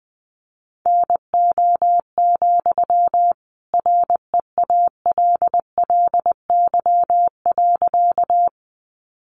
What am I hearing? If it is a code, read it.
Morse code, heard as a keyed tone, about 20 wpm, 700 Hz. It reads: NO, REALLY.
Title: Sentences from Top 600 Words 20wpm